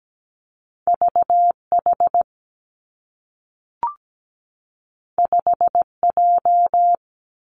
Morse code, heard as a keyed tone, about 17 wpm, 700 Hz. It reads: V H 5 J
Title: Two Challenging Letter-Number Combinations 17wpm